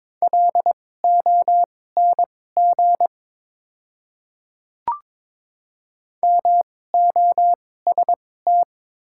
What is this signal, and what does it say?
Morse code, heard as a keyed tone, about 22 wpm, spaced as for 16 wpm, 700 Hz. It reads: LONG MOST
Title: 2x Character Spacing - Top 100 Words - 22wpm